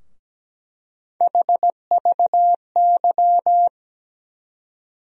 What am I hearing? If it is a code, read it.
Morse code, heard as a keyed tone, about 17 wpm, 700 Hz. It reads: HVY
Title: USA Call Sign Postfixes - 3 Letters 17wpm